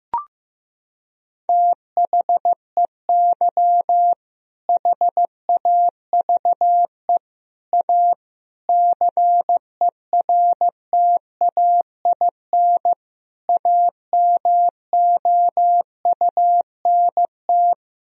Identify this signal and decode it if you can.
Morse code, heard as a keyed tone, about 14 wpm, 700 Hz. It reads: THEY HAVE A CERTAIN AMOUNT
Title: Sentences from Top 1000 Words - Review Part 4 - 15wpm